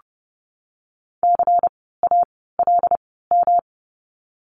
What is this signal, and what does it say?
Morse code, heard as a keyed tone, about 30 wpm, spaced as for 17 wpm, 700 Hz. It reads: CALM
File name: course / 30:10wpm / 108 Lesson 18f - Letter m - Words 30wpm